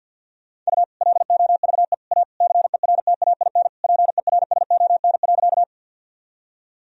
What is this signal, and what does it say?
Morse code, heard as a keyed tone, about 50 wpm, 700 Hz. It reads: U LOVE A CERTAIN PERSON.
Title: Sentences from Top 500 Words and Abbreviations 50wpm